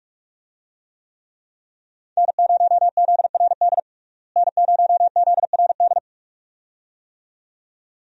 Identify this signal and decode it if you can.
Morse code, heard as a keyed tone, about 45 wpm, 700 Hz. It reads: N0ZRD N0ZRD
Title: USA Call Signs - 5 Characters - Repeated 2x - 45wpm